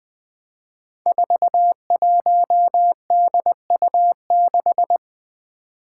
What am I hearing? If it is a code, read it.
Morse code, heard as a keyed tone, about 20 wpm, 700 Hz. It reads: 41DU6
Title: Five Challenging Letter-Number Combinations 20wpm